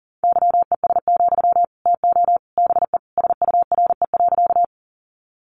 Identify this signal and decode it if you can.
Morse code, heard as a keyed tone, about 40 wpm, 700 Hz. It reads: YES, TO BE SURE.
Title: Sentences from Top 500 Words 40wpm